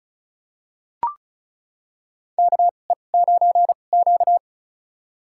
Morse code, heard as a keyed tone, about 34 wpm, spaced as for 25 wpm, 700 Hz. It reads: KE9Q
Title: USA Call Signs - 4 Characters - 2x Character Spacing - 35wpm